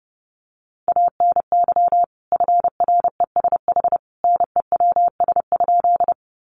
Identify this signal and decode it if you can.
Morse code, heard as a keyed tone, about 30 wpm, 700 Hz. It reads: ANY FRESH NEWS?
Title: Sentences from Top 1200 Words 30wpm